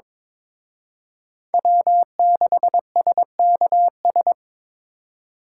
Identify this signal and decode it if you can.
Morse code, heard as a keyed tone, about 26 wpm, 700 Hz. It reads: W6SKS
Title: Speed Racing - USA 1x3 Call Signs 22wpm